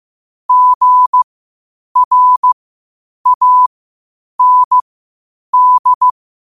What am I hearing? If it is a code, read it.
Morse code, heard as a keyed tone, about 15 wpm, spaced as for 8 wpm, 1000 Hz. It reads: GRAND
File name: qotdfiles / 8wpm